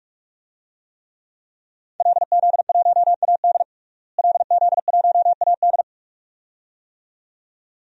Morse code, heard as a keyed tone, about 41 wpm, 700 Hz. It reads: LZ1AD LZ1AD
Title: DXCC-100 Call Signs - 5 Characters - Repeated 2x - 45wpm